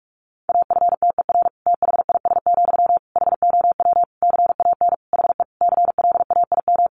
Morse code, heard as a keyed tone, about 44 wpm, 700 Hz. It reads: AFTER THIS, HOW CAN HE CLAIM
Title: Sentences from Top 1500 Words 45wpm